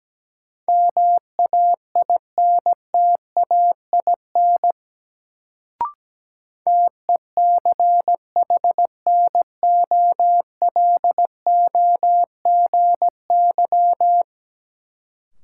Morse code, heard as a keyed tone, about 17 wpm, 700 Hz. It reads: MAINTAIN TECHNOLOGY
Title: Sentences from Top 1000 Words - Review Part 5 - New Word Drill - 17wpm